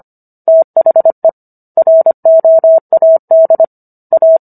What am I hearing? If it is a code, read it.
Morse code, heard as a keyed tone, about 25 wpm, 650 Hz. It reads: THE ROAD A